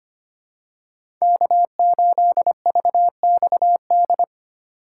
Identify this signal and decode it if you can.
Morse code, heard as a keyed tone, about 25 wpm, 700 Hz. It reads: K8VXD
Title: USA Call Signs - 5 Characters - 2x Character Spacing - 25wpm